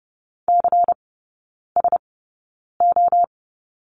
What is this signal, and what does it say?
Morse code, heard as a keyed tone, about 30 wpm, spaced as for 9 wpm, 690 Hz. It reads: CSO
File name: Lesson 16e - Letter c - Three Characters 30wpm